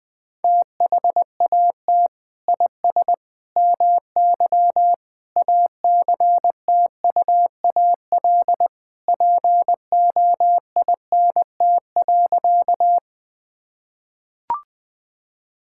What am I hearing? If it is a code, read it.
Morse code, heard as a keyed tone, about 21 wpm, 700 Hz. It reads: THAT IS MY ACTUAL POINT.
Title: Sentences from Top 1000 Words - Review Part 5 - 20wpm